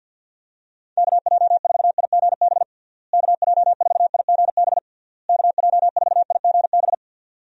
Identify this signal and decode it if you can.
Morse code, heard as a keyed tone, about 50 wpm, 700 Hz. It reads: KJ4IGB KJ4IGB KJ4IGB
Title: USA Call Signs - 6 Characters - Repeated 3x - 50wpm